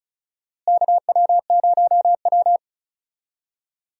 Morse code, heard as a keyed tone, about 36 wpm, 700 Hz. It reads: KW0W
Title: USA Call Signs - 4 Characters - Repeated 2x - 35wpm